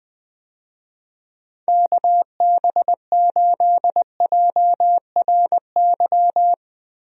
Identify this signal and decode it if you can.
Morse code, heard as a keyed tone, about 20 wpm, 700 Hz. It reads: KB8JRY
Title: USA Call Signs - 6 Characters - Repeated 2x - 20wpm